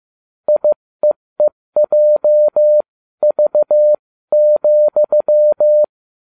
Morse code, heard as a keyed tone, about 15 wpm, spaced as for 13 wpm, 600 Hz. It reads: IEEJV,
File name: cw / ubacw20 3